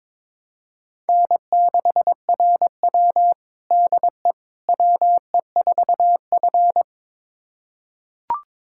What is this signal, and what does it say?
Morse code, heard as a keyed tone, about 22 wpm, 700 Hz. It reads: N6RW DE WE4F
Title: USA Callsign DE Callsign - 4 Characters - 22wpm